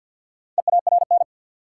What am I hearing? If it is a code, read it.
Morse code, heard as a keyed tone, about 52 wpm, 700 Hz. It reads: EARN
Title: Sentences from Top 1500 Words - New Word Drill Alt - 50wpm